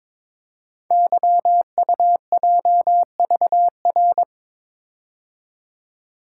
Morse code, heard as a keyed tone, about 22 wpm, 700 Hz. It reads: YUJVR
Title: Code Groups - 5 Letters - 22wpm